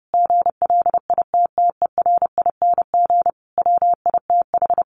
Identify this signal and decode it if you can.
Morse code, heard as a keyed tone, about 30 wpm, 700 Hz. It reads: GLITTERING WITH
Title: A Sound Of Thunder 30wpm